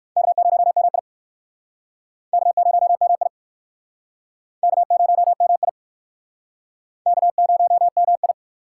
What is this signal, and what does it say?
Morse code, heard as a keyed tone, about 59 wpm, 700 Hz. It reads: K0MI K0MI K0MI K0MI